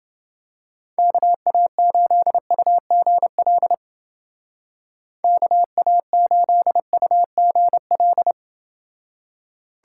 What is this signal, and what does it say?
Morse code, heard as a keyed tone, about 31 wpm, 700 Hz. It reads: KA8UGL KA8UGL